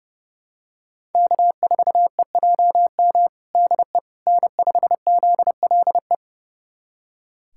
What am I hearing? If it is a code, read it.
Morse code, heard as a keyed tone, about 30 wpm, 700 Hz. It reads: K4EJM DE N5ZLE
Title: USA Callsign DE Callsign - 5 Characters - 30wpm